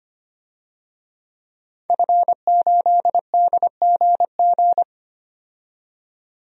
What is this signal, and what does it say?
Morse code, heard as a keyed tone, about 25 wpm, 700 Hz. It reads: F8DGG
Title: DXCC-100 Call Signs - 5 Characters - 25wpm